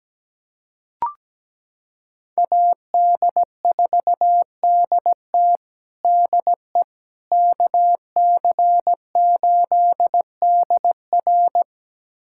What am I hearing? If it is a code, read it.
Morse code, heard as a keyed tone, about 17 wpm, 700 Hz. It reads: AD4DT DE KC8DR
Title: USA Callsign DE Callsign - 5 Characters - 17wpm